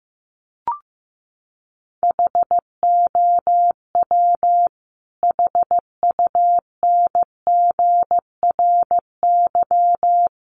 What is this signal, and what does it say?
Morse code, heard as a keyed tone, about 14 wpm, 700 Hz. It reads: HOW HUNGRY
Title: Sentences from Top 1400 Words 15wpm